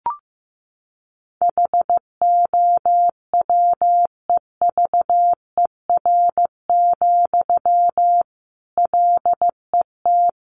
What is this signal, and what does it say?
Morse code, heard as a keyed tone, about 15 wpm, 700 Hz. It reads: HOWEVER, LET
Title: Sentences from Top 600 Words 15wpm